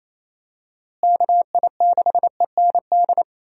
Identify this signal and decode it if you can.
Morse code, heard as a keyed tone, about 28 wpm, 700 Hz. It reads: KI6END
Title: USA Call Signs - 6 Characters - Repeated 3x - 28wpm